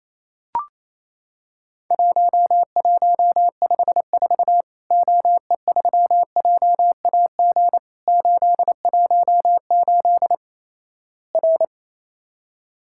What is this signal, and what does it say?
Morse code, heard as a keyed tone, about 28 wpm, 700 Hz. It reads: 1154 OE3JAG 818 R
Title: Worked All Europe DX Contest - QTC Exchange - 28wpm